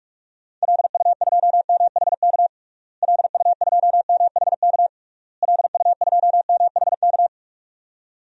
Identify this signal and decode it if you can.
Morse code, heard as a keyed tone, about 41 wpm, 700 Hz. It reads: LU2MHK LU2MHK LU2MHK
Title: DXCC-100 Call Signs - 6 Characters - Repeated 3x - 45wpm